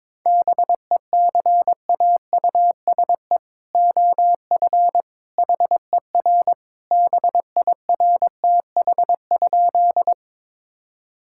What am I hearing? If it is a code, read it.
Morse code, heard as a keyed tone, about 22 wpm, 700 Hz. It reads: BECAUSE OF HER BIRTH?
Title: Sentences from Top 1000 Words - Review Part 5 - 22wpm